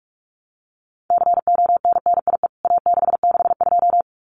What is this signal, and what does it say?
Morse code, heard as a keyed tone, about 46 wpm, 700 Hz. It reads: CONNIE A662
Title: CWT Exchange 45wpm